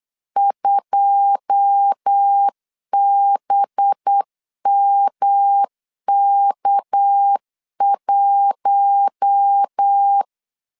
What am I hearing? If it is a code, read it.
Morse code, heard as a keyed tone, about 8 wpm, 800 Hz. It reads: 2BMK1
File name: LEZIONI / 9 2 8wpm 3437